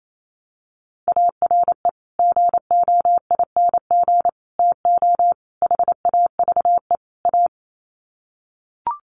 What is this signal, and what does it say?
Morse code, heard as a keyed tone, about 28 wpm, 700 Hz. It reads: ARE GOING TO HAVE A
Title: Sets of 5 Words Encore2 28wpm